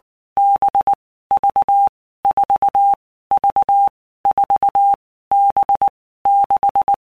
Morse code, heard as a keyed tone, about 19 wpm, 800 Hz. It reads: B V 4 V 4 B 6
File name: Basic Letter Groups By Blocks / Groups@18 / VB46@18